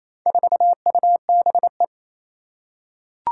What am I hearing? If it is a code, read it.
Morse code, heard as a keyed tone, about 27 wpm, 700 Hz. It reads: 4UBE
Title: Four Challenging Letter-Number Combinations 28wpm